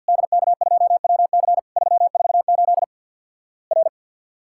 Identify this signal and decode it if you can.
Morse code, heard as a keyed tone, about 50 wpm, 700 Hz. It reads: DK2WC 348 R
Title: Worked All Europe DX Contest - QTC Exchange with Repeats - 50wpm